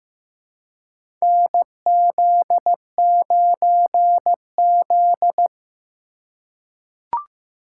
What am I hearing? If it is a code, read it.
Morse code, heard as a keyed tone, about 15 wpm, 700 Hz. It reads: NZ9Z
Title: USA Call Signs - 4 Characters - 3x Character Spacing - 15wpm